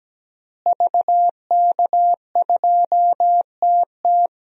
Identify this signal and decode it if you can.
Morse code, heard as a keyed tone, about 17 wpm, 700 Hz. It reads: VK2TT
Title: DXCC-100 Call Signs - 5 Characters - Repeated 3x - 17wpm